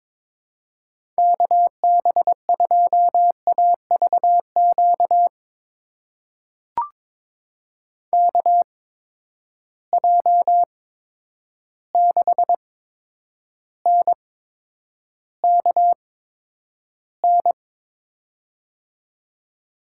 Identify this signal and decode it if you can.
Morse code, heard as a keyed tone, about 22 wpm, spaced as for 9 wpm, 700 Hz. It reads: KB2AVQ KJ6NKN
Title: USA Call Signs - 6 Characters - 8x Character Spacing - 22wpm